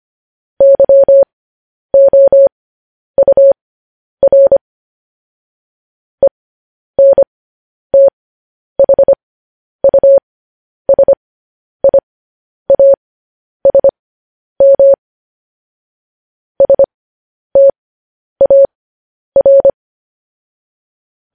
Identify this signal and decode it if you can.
Morse code, heard as a keyed tone, about 25 wpm, spaced as for 10 wpm, 550 Hz. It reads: YOUR ENTHUSIASM STAR